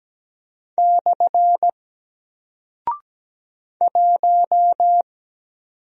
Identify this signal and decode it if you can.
Morse code, heard as a keyed tone, about 17 wpm, 700 Hz. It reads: / 1
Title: Single Character (Letter:Number:Punctuation:Prosign) 17wpm